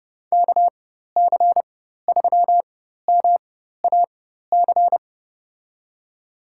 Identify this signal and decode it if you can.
Morse code, heard as a keyed tone, about 30 wpm, spaced as for 14 wpm, 700 Hz. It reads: KC3MAC
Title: USA Call Signs - 6 Characters - 4x Character Spacing - 30wpm